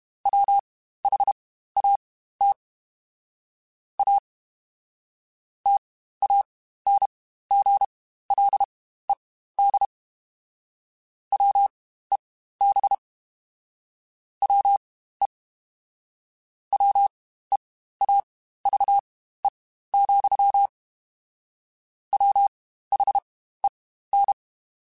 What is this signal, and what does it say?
Morse code, heard as a keyed tone, about 32 wpm, spaced as for 15 wpm, 800 Hz. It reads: WHAT A TANGLED WEB WE WEAVE, WHEN
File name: Quotes Spaced / Quotes Spaced-mp3s / Deceive sp @30